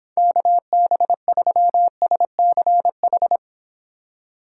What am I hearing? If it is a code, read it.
Morse code, heard as a keyed tone, about 23 wpm, 700 Hz. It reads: KB3SCH